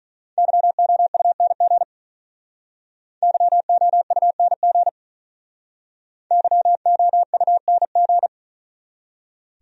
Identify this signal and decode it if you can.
Morse code, heard as a keyed tone, about 46 wpm, 700 Hz. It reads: YOUNG YOUNG YOUNG